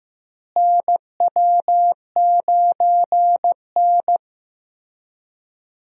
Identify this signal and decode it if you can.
Morse code, heard as a keyed tone, about 15 wpm, 700 Hz. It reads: NW9N
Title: USA Call Signs - 4 Characters - Repeated 3x - 15wpm